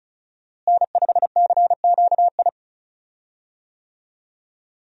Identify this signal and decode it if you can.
Morse code, heard as a keyed tone, about 34 wpm, 700 Hz. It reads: N5CQI
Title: USA Call Signs - 5 Characters - Repeated 3x - 35wpm